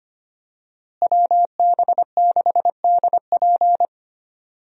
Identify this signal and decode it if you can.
Morse code, heard as a keyed tone, about 25 wpm, 700 Hz. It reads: WB6DP
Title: USA Call Signs - 5 Characters - Repeated 3x - 25wpm